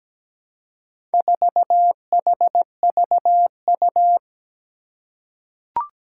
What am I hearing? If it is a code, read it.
Morse code, heard as a keyed tone, about 17 wpm, 700 Hz. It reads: 4HVU
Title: Four Challenging Letter-Number Combinations 17wpm